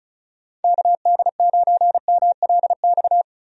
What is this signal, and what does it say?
Morse code, heard as a keyed tone, about 34 wpm, 700 Hz. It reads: KD9MLX
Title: USA Call Signs - 6 Characters - Repeated 3x - 35wpm